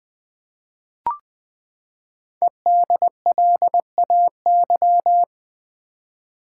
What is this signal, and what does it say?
Morse code, heard as a keyed tone, about 20 wpm, 700 Hz. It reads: EDLAY
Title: Code Groups - 5 Letters - 20wpm